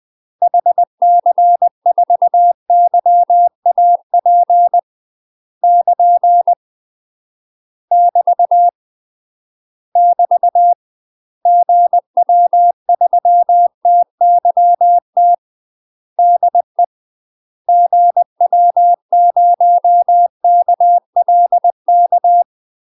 Morse code, heard as a keyed tone, about 20 wpm, 700 Hz. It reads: HC4YAP <KN> <BT> <BT> GW3TYT DE GW0KLK
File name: UK MASTER QSOs TRAINING / 20WPM / UK MASTER QSOs 0024